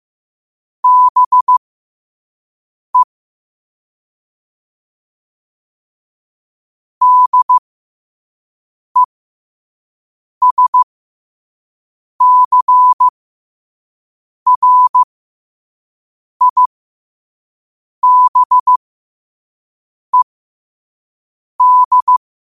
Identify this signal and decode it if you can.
Morse code, heard as a keyed tone, about 15 wpm, spaced as for 5 wpm, 1000 Hz. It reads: BE DESCRIBED
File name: qotdfiles / 5wpm